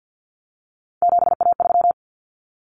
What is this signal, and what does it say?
Morse code, heard as a keyed tone, about 50 wpm, 700 Hz. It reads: 7A3
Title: Common Cut Numbers - 3-Digit 50wpm